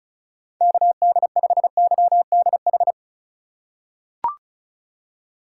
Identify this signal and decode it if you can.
Morse code, heard as a keyed tone, about 34 wpm, 700 Hz. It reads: KD5YDH
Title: USA Call Signs - 6 Characters - 4x Character Spacing - 35wpm